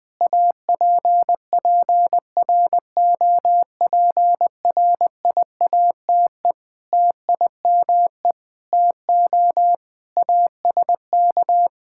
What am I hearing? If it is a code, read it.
Morse code, heard as a keyed tone, about 21 wpm, 700 Hz. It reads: APPROPRIATE TIME TO ASK
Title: Sentences from Top 1000 Words - Review Part 5 - 20wpm